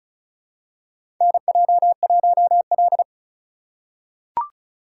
Morse code, heard as a keyed tone, about 36 wpm, 700 Hz. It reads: NJ1L
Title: USA Call Signs - 4 Characters - 3x Character Spacing - 35wpm